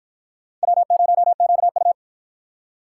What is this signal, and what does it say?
Morse code, heard as a keyed tone, about 49 wpm, 700 Hz. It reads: W0QU